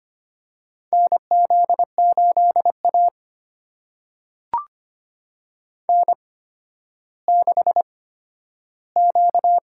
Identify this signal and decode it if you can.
Morse code, heard as a keyed tone, about 25 wpm, spaced as for 12 wpm, 700 Hz. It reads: NZ8A N6Q
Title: USA Call Signs - 4 Characters - 8x Character Spacing - 25wpm